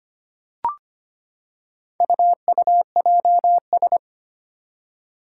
Morse code, heard as a keyed tone, about 25 wpm, 700 Hz. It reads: UUJS
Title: Four Challenging Letter-Number Combinations 25wpm